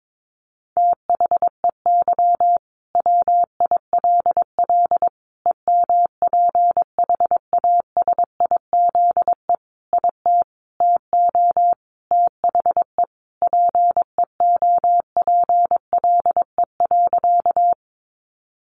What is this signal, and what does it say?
Morse code, heard as a keyed tone, about 22 wpm, 700 Hz. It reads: THEY WILL EMPHASIZE IT TO THE PEOPLE.
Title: Sentences from Top 1500 Words 22wpm